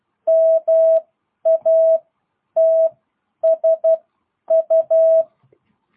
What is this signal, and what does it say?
Morse code, heard as a keyed tone, about 12 wpm, spaced as for 9 wpm, 660 Hz. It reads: MATSU
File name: jh8cbh / 2022cw